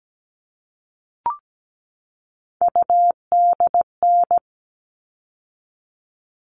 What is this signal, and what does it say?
Morse code, heard as a keyed tone, about 17 wpm, 700 Hz. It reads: UDN
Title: All Cut Numbers - 3-Digit 17wpm